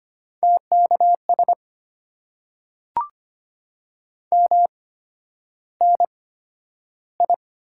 Morse code, heard as a keyed tone, about 25 wpm, spaced as for 11 wpm, 700 Hz. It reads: TKS MNI
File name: Top 90 QSO Elements - 8x Character Spacing - 25wpm